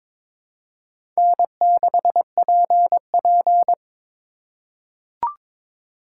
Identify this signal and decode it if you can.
Morse code, heard as a keyed tone, about 22 wpm, 700 Hz. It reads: N6PP
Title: USA Call Signs - 4 Characters - 4x Character Spacing - 22wpm